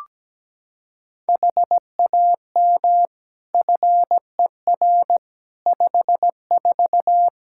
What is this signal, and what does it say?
Morse code, heard as a keyed tone, about 17 wpm, 700 Hz. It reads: HAM FER 54
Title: Practice QSO - Exchange 3 - More Info - 17wpm